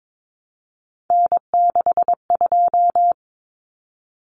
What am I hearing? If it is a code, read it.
Morse code, heard as a keyed tone, about 22 wpm, 700 Hz. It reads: N62
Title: Common Cut Numbers - 3-Digit 22wpm